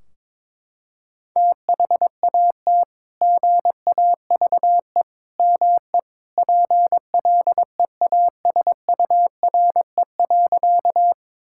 Morse code, heard as a keyed tone, about 22 wpm, 700 Hz. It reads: THAT GAVE ME PLEASURE.
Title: Sentences from Top 1600 Words 22wpm